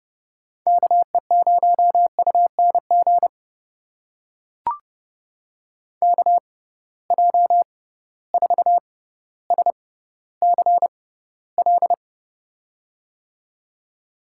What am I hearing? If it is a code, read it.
Morse code, heard as a keyed tone, about 30 wpm, spaced as for 15 wpm, 700 Hz. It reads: KE0UNG KJ4SCL
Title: USA Call Signs - 6 Characters - 6x Character Spacing - 30wpm